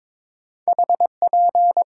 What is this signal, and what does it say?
Morse code, heard as a keyed tone, about 22 wpm, 700 Hz. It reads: HP
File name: Top 90 QSO Elements - 6x Character Spacing - 22wpm